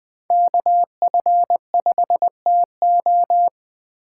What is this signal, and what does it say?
Morse code, heard as a keyed tone, about 20 wpm, 700 Hz. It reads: KF5TO
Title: USA Call Signs - 5 Characters - 2x Character Spacing - 20wpm